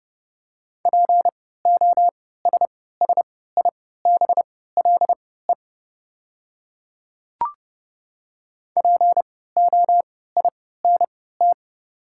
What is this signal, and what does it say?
Morse code, heard as a keyed tone, about 30 wpm, spaced as for 17 wpm, 700 Hz. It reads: POSSIBLE POINT
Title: 3x Character Spacing - Top 500 Words - 30wpm